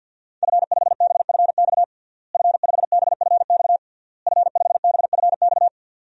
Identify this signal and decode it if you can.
Morse code, heard as a keyed tone, about 50 wpm, 700 Hz. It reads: F5BFX F5BFX F5BFX
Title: DXCC-100 Call Signs - 5 Characters - Repeated 3x - 50wpm